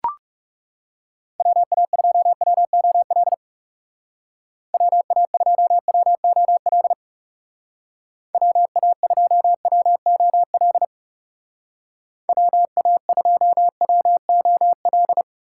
Morse code, heard as a keyed tone, about 35 wpm, 700 Hz. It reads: WA2WOL WA2WOL WA2WOL WA2WOL
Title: Speed Racing - USA 2x3 Call Signs 30wpm